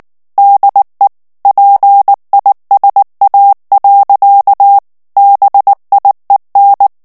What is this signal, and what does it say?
Morse code, heard as a keyed tone, about 19 wpm, 800 Hz. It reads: DE PISA. BIEN
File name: FRANCAIS TEXTE CLAIR 1 VITESSE 18 MOTS MINUTES